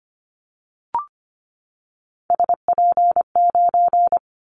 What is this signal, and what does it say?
Morse code, heard as a keyed tone, about 25 wpm, 700 Hz. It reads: SP9
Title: Callsign Prefixes for Top 25 DXCC - 25wpm